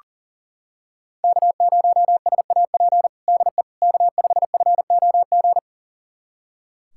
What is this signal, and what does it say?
Morse code, heard as a keyed tone, about 40 wpm, 700 Hz. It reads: K0SAP DE K5FOG
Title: USA Callsign DE Callsign - 5 Characters - 40wpm